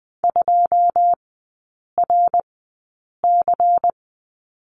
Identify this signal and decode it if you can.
Morse code, heard as a keyed tone, about 20 wpm, spaced as for 8 wpm, 700 Hz. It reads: 2RC
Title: 093 Lesson 16e - Letter c - Three Characters 20wpm